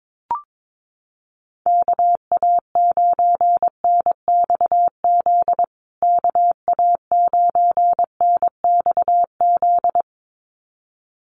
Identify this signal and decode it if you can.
Morse code, heard as a keyed tone, about 22 wpm, 700 Hz. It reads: KA9NXZ KA9NXZ
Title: USA Call Signs Repeated 2x 22wpm